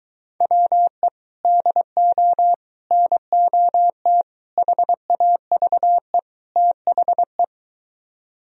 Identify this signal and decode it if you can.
Morse code, heard as a keyed tone, about 23 wpm, 700 Hz. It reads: WE DO NOT HAVE THE